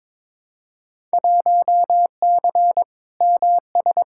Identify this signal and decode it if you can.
Morse code, heard as a keyed tone, about 22 wpm, 700 Hz. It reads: 1C MS
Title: Field Day Exchanges Encore - 22wpm